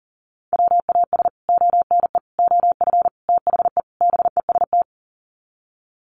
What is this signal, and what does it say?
Morse code, heard as a keyed tone, about 40 wpm, 700 Hz. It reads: WAS ONE OF THE BEST
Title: Sets of 5 Words Encore2 40wpm